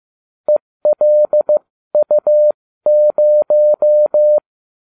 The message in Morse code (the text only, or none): ELU0